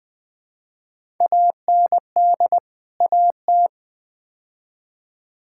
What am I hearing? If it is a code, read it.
Morse code, heard as a keyed tone, about 20 wpm, 700 Hz. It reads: AND AT
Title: Build A Phrase - 5-Word Phrases - 20wpm